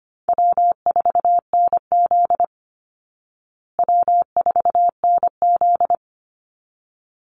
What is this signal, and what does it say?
Morse code, heard as a keyed tone, about 25 wpm, 700 Hz. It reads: W4NZ W4NZ
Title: Call Signs Repeated 3x - 25wpm